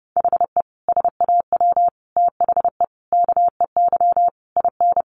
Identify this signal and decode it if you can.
Morse code, heard as a keyed tone, about 30 wpm, 700 Hz. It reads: HE SAW THE KEY IN
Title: Sentences from Top 1300 Words 30wpm